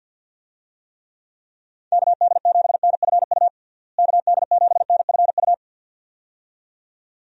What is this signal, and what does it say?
Morse code, heard as a keyed tone, about 47 wpm, 700 Hz. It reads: KD7NFU KD7NFU
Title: USA Call Signs - 6 Characters - Repeated 2x - 50wpm